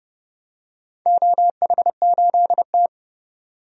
Morse code, heard as a keyed tone, about 30 wpm, 700 Hz. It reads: OH8T
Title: DXCC-100 Call Signs - 4 Characters - 30wpm